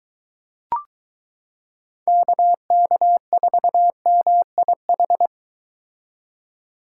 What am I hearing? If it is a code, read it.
Morse code, heard as a keyed tone, about 22 wpm, 700 Hz. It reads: KK4MIH